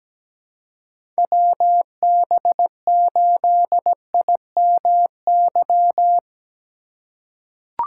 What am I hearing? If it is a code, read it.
Morse code, heard as a keyed tone, about 17 wpm, 700 Hz. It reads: WB8IMY
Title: K1USN SST QSO Practice - 17wpm